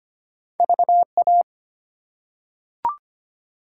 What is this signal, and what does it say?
Morse code, heard as a keyed tone, about 25 wpm, 700 Hz. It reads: VA
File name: practice / 25wpm / ARRL Sections v2 - 25wpm